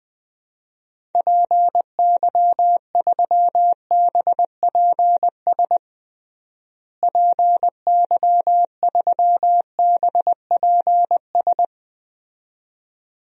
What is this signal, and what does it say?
Morse code, heard as a keyed tone, about 20 wpm, 700 Hz. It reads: PY3BPS PY3BPS
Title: DXCC-100 Call Signs - 6 Characters - Repeated 2x - 20wpm